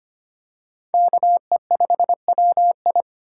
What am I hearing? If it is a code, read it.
Morse code, heard as a keyed tone, about 25 wpm, 700 Hz. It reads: KE5WI
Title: USA Call Signs - 5 Characters - 4x Character Spacing - 25wpm